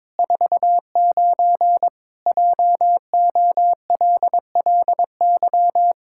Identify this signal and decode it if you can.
Morse code, heard as a keyed tone, about 22 wpm, 700 Hz. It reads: 49 JOLLY
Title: Pangram Challenge v2 - 22wpm